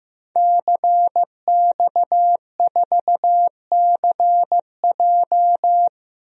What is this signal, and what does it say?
Morse code, heard as a keyed tone, about 15 wpm, 700 Hz. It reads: CX4CJ
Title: DXCC-100 Call Signs - 5 Characters - Repeated 3x - 15wpm